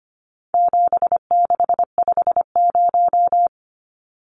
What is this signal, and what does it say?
Morse code, heard as a keyed tone, about 25 wpm, 700 Hz. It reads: 7650
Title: Numbers - 4-Digit 25wpm